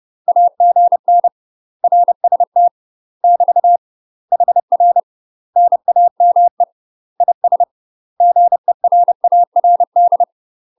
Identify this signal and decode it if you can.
Morse code, heard as a keyed tone, about 30 wpm, 700 Hz. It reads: AGN RST <BT> HR NAME IS GERARD
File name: UK MASTER QSOs TRAINING / 30WPM / UK MASTER QSOs 0009